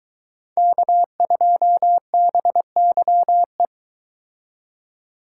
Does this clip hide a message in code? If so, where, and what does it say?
Yes: Morse code from 0.6 to 3.7 s, heard as a keyed tone, about 22 wpm, 700 Hz. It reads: K2BYE